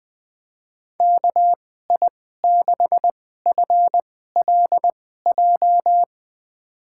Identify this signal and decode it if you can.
Morse code, heard as a keyed tone, about 20 wpm, spaced as for 14 wpm, 700 Hz. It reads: KI6FLJ
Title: USA Call Signs - 6 Characters - 2x Character Spacing - 20wpm